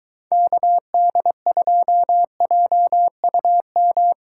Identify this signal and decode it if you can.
Morse code, heard as a keyed tone, about 22 wpm, 700 Hz. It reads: KD2JUM